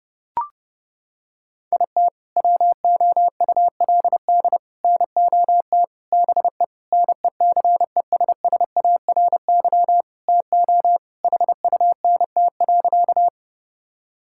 Morse code, heard as a keyed tone, about 30 wpm, 700 Hz. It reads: IT WOULD NOT BE NECESSARY TO HUNT.
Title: Sentences from Top 1600 Words 30wpm